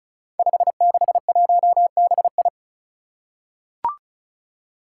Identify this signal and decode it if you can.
Morse code, heard as a keyed tone, about 34 wpm, 700 Hz. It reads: 561BI
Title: Five Challenging Letter-Number Combinations 35wpm